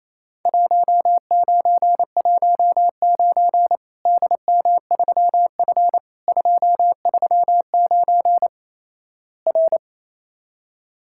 Morse code, heard as a keyed tone, about 28 wpm, 700 Hz. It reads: 1919 DM3F 239 R
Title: Worked All Europe DX Contest - QTC Exchange - 28wpm